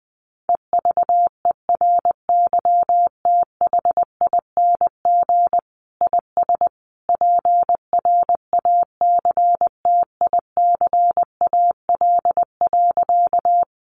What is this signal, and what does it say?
Morse code, heard as a keyed tone, about 20 wpm, 700 Hz. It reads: EVERYTHING IS PRACTICAL.
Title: Sentences from Top 1400 Words 20wpm